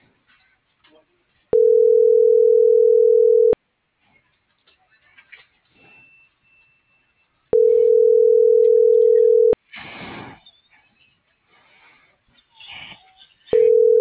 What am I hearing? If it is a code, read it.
Ringback tone (North America), 440 + 480 Hz.